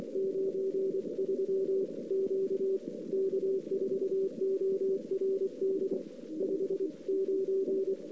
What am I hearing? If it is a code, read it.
Morse code, heard as a keyed tone, about 30 wpm, 410 Hz. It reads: 93 Q K4ORD 59
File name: CW / 2008-ss-cw-k5tr-op-n5rz -03:00